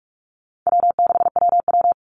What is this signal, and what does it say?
Morse code, heard as a keyed tone, about 44 wpm, 700 Hz. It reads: W6WW
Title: NA SOTA Chaser Callsigns 45wpm